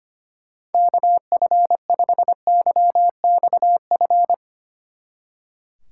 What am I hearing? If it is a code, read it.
Morse code, heard as a keyed tone, about 28 wpm, 700 Hz. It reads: KF5YXF
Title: Speed Racing - USA 2x3 Call Signs 25wpm